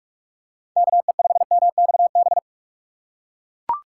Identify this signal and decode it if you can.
Morse code, heard as a keyed tone, about 42 wpm, 700 Hz. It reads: KE5MXB